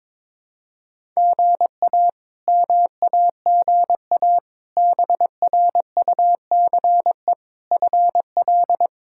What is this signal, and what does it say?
Morse code, heard as a keyed tone, about 21 wpm, 700 Hz. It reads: GA MAGA BRUCE FL
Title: K1USN SST QSO Practice - 22wpm